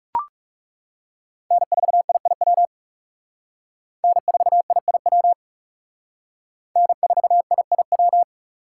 Morse code, heard as a keyed tone, about 37 wpm, 700 Hz. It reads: N4IIW N4IIW N4IIW
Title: Speed Racing - USA 1x3 Call Signs 30wpm